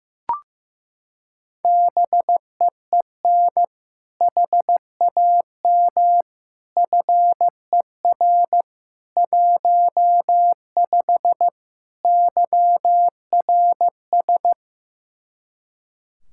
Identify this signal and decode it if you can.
Morse code, heard as a keyed tone, about 16 wpm, 700 Hz. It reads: BEEN HAM FER 15 YRS
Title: Practice QSO - Exchange 3 - More Info - 15wpm